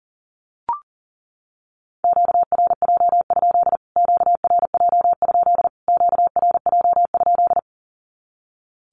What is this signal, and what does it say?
Morse code, heard as a keyed tone, about 40 wpm, 700 Hz. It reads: QRJ? QRJ? QRJ?
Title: Q-Codes Repeated 3x 40wpm